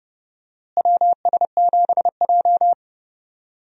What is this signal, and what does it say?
Morse code, heard as a keyed tone, about 32 wpm, 700 Hz. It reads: WS7J
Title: USA Call Signs - 4 Characters - Repeated 3x - 30wpm